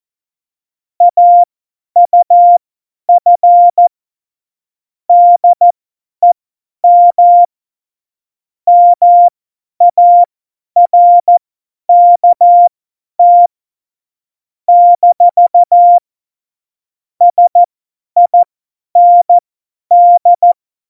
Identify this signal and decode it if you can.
Morse code, heard as a keyed tone, about 14 wpm, spaced as for 10 wpm, 700 Hz. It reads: AUF DEM MARKT - SIND
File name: Morsekurs / Witze / Witz 29. August 17, Tempo 50 F70